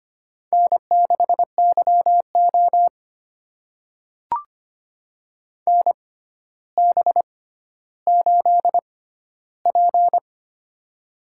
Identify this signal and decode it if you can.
Morse code, heard as a keyed tone, about 25 wpm, spaced as for 13 wpm, 700 Hz. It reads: N6YO NB8P
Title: USA Call Signs - 4 Characters - 6x Character Spacing - 25wpm